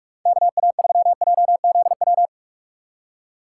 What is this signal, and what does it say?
Morse code, heard as a keyed tone, about 44 wpm, 700 Hz. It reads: KA3JZW
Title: USA Call Signs - 6 Characters - 6x Character Spacing - 45wpm